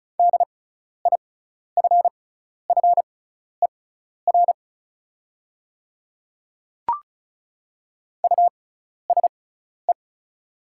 Morse code, heard as a keyed tone, about 34 wpm, spaced as for 12 wpm, 690 Hz. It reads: DIFFER USE
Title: 6x Character Spacing - Top 200 Words - 35wpm